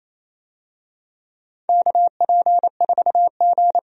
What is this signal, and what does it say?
Morse code, heard as a keyed tone, about 28 wpm, 700 Hz. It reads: KP4G
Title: DXCC-100 Call Signs - 4 Characters - 28wpm